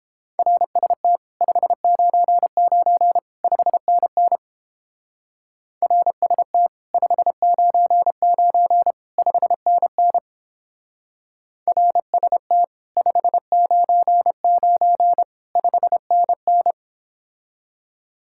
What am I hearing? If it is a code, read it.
Morse code, heard as a keyed tone, about 29 wpm, 700 Hz. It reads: RST 599 5NN RST 599 5NN RST 599 5NN